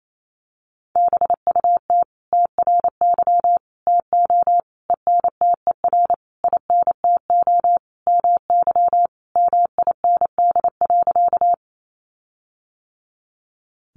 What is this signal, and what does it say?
Morse code, heard as a keyed tone, about 28 wpm, 700 Hz. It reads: BUT TRY TO ENTER INTO MY MIND.
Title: Sentences from Top 1300 Words 28wpm